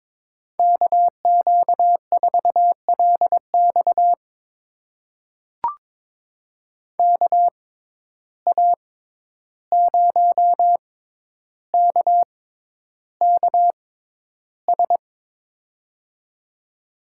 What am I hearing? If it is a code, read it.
Morse code, heard as a keyed tone, about 22 wpm, spaced as for 11 wpm, 700 Hz. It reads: KQ4LX KA0KKS
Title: USA Call Signs - 6x Character Spacing - 22wpm